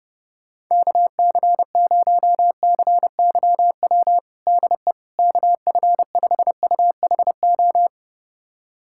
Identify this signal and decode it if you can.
Morse code, heard as a keyed tone, about 32 wpm, 700 Hz. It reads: KC0CYW DE KF5UHO
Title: USA Callsign DE Callsign - 6 Characters - 30wpm